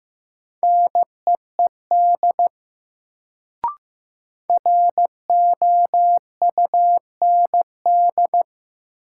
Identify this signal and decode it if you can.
Morse code, heard as a keyed tone, about 15 wpm, 700 Hz. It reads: NEED ROUND
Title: Sentences from Top 200 Words - New Word Drill - 15wpm